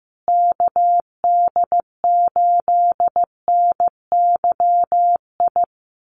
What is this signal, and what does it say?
Morse code, heard as a keyed tone, about 15 wpm, 700 Hz. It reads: KD8NYI
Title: USA Call Signs Repeated 3x 15wpm